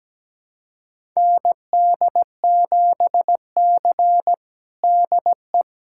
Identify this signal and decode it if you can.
Morse code, heard as a keyed tone, about 17 wpm, 700 Hz. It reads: ND7C DE
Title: USA Callsign DE Callsign - 4 Characters - 17wpm